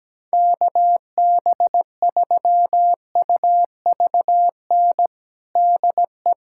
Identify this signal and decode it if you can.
Morse code, heard as a keyed tone, about 17 wpm, 700 Hz. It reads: KB3UVN DE
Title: USA Callsign DE Callsign 17wpm